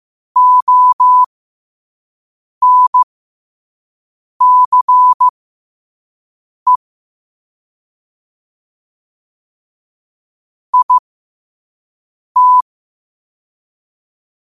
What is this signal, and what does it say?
Morse code, heard as a keyed tone, about 15 wpm, spaced as for 5 wpm, 1000 Hz. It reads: ONCE IT
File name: qotdfiles / 5wpm